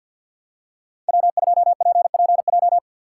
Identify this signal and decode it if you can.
Morse code, heard as a keyed tone, about 53 wpm, 700 Hz. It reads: W2PPJ
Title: USA Call Signs - 5 Characters - 2x Character Spacing - 50wpm